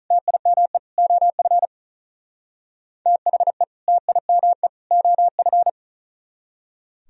Morse code, heard as a keyed tone, about 45 wpm, 700 Hz. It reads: TIME OF THE TIME OF